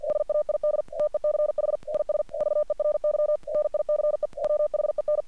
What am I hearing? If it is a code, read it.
Morse code, heard as a keyed tone, about 48 wpm, 600 Hz. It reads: SAID TECH IS VERY NICE WHEN
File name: archived / CRSnet